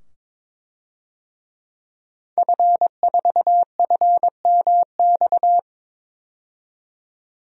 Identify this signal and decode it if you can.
Morse code, heard as a keyed tone, about 22 wpm, 700 Hz. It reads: F4FMX
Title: DXCC-100 Call Signs - 5 Characters - 22wpm